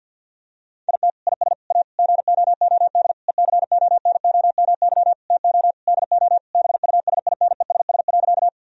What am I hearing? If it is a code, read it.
Morse code, heard as a keyed tone, about 50 wpm, 700 Hz. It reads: IT IS A GOOD ECONOMY TO DO BUSINESS.
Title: Sentences from Top 1000 Words - Review Part 3 - 50wpm